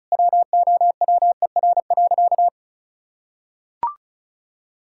Morse code, heard as a keyed tone, about 34 wpm, 700 Hz. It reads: WOWER.
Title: Pangram Challenge v2 - 35wpm